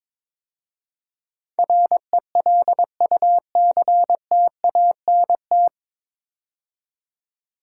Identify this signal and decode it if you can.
Morse code, heard as a keyed tone, about 22 wpm, 700 Hz. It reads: RELUCTANT
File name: 9 Letter Words - First 3 Letters Given - 22wpm